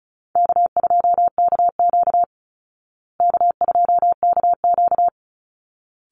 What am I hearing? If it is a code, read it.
Morse code, heard as a keyed tone, about 34 wpm, 700 Hz. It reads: K2KQ K2KQ
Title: Call Signs Repeated 3x - 35wpm